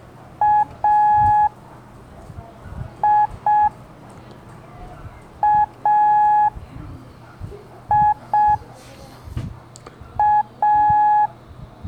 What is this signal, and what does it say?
Morse code, heard as a keyed tone, about 6 wpm, 840 Hz. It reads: A I A I A
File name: blog / LETRAS COM SOM SEMELHANTE MORSE